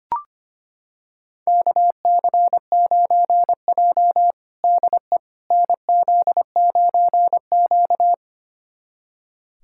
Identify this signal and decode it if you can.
Morse code, heard as a keyed tone, about 25 wpm, 700 Hz. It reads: KC9J DE NZ9Q
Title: USA Callsign DE Callsign - 4 Characters - 25wpm